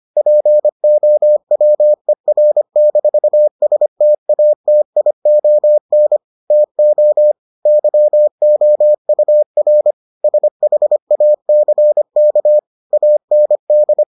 Morse code, heard as a keyed tone, about 25 wpm, 600 Hz. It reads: POWER-STATION TO YOUR SHACK AND